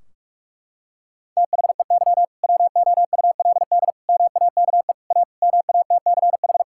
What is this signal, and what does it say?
Morse code, heard as a keyed tone, about 47 wpm, 700 Hz. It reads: THEY WOULD MAKE A MATCH
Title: Sentences from Top 1600 Words 45wpm